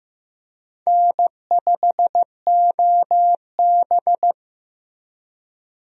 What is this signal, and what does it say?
Morse code, heard as a keyed tone, about 16 wpm, 700 Hz. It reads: N5OB
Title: USA Call Signs - 4 Characters - Repeated 2x - 15wpm